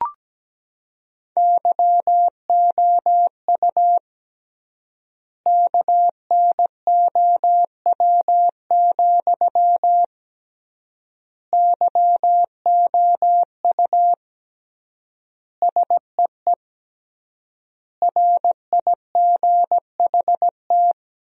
Morse code, heard as a keyed tone, about 17 wpm, 700 Hz. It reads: YOU KNOW, YOU SEE RIGHT
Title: 3x Word-Spacing - Sentences from Top 200 Words - 17wpm